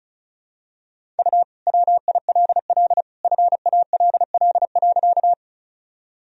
Morse code, heard as a keyed tone, about 36 wpm, 700 Hz. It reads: U WILL FALL.
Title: Sentences from Top 500 Words and Abbreviations 35wpm